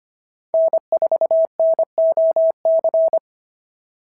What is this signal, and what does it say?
Morse code, heard as a keyed tone, about 24 wpm, 650 Hz. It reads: N4NOC
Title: SOTA QSO Practice 25wpm